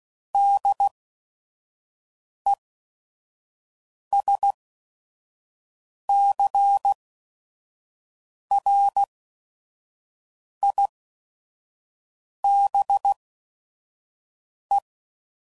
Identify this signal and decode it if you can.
Morse code, heard as a keyed tone, about 16 wpm, spaced as for 5 wpm, 800 Hz. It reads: DESCRIBE
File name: CW Code Groups @5WPM / Technical Subjects / Reactance